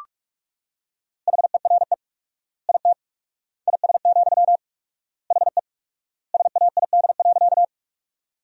Morse code, heard as a keyed tone, about 47 wpm, 700 Hz. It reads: HERE IT IS, HE SAID.
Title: 4x Word-Spacing - Sentences from Top 200 Words - 45wpm